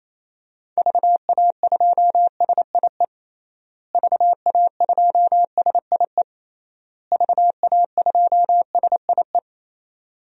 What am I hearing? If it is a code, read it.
Morse code, heard as a keyed tone, about 28 wpm, 700 Hz. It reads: VA2SIE VA2SIE VA2SIE
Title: DXCC-100 Call Signs - 6 Characters - Repeated 3x - 28wpm